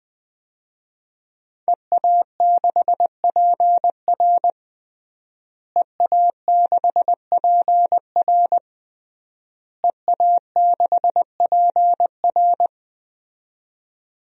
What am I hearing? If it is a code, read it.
Morse code, heard as a keyed tone, about 20 wpm, 700 Hz. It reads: EA6PR EA6PR EA6PR
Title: DXCC-100 Call Signs - 5 Characters - Repeated 3x - 20wpm